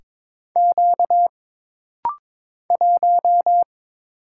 Morse code, heard as a keyed tone, about 22 wpm, 700 Hz. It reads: Q 1
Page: This rapid-fire timing is geared to push your ICR (Instant Character Recognition) skills.